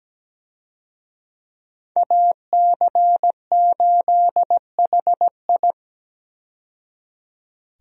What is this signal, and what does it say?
Morse code, heard as a keyed tone, about 17 wpm, 700 Hz. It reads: AC8HI
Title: USA Call Signs - 5 Characters - 17wpm